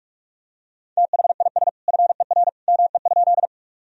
Morse code, heard as a keyed tone, about 45 wpm, 700 Hz. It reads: THIS FER ME?
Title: Sentences from Top 200 Words and Abbreviations 45wpm